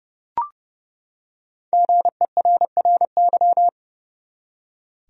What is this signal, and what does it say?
Morse code, heard as a keyed tone, about 32 wpm, 700 Hz. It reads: GERRY
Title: USA Common Names - Max 5 Letters - 30wpm